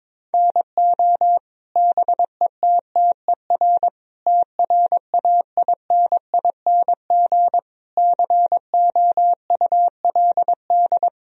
Morse code, heard as a keyed tone, about 22 wpm, 700 Hz. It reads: NO BETTER TRAINING COULD
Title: Sentences from Top 1000 Words - Review Part 4 - 22wpm